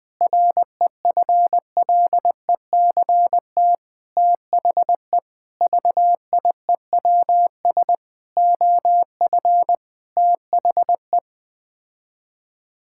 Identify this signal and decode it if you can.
Morse code, heard as a keyed tone, about 20 wpm, 700 Hz. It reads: REFLECT THE VIEWS OF THE
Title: Sets of 8 Words - 20wpm